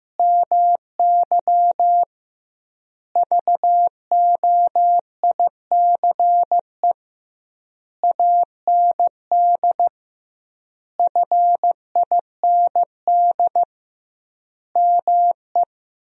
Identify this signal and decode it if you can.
Morse code, heard as a keyed tone, about 15 wpm, 700 Hz. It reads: MY VOICE AND FIND ME
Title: 2x Word-Spacing - Sentences from Top 500 Words - 15wpm